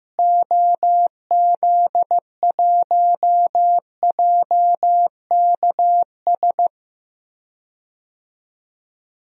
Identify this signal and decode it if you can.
Morse code, heard as a keyed tone, about 16 wpm, 700 Hz. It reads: OZ1JKS
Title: DXCC-100 Call Signs - 6 Characters - Repeated 2x - 15wpm